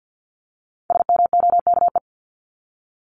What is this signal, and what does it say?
Morse code, heard as a keyed tone, about 48 wpm, 700 Hz. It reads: SMOKE
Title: Top 1501-1750 Words 50wpm